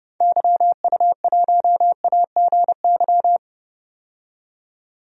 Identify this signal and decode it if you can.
Morse code, heard as a keyed tone, about 30 wpm, 700 Hz. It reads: YU1AGY
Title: DXCC-100 Call Signs - 6 Characters - 30wpm